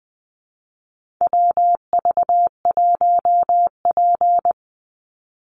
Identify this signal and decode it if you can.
Morse code, heard as a keyed tone, about 20 wpm, 700 Hz. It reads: WV1P
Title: USA Call Signs 20wpm